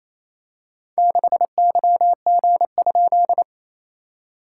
Morse code, heard as a keyed tone, about 29 wpm, 690 Hz. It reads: 6YG?
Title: SOTA QSO Practice 28wpm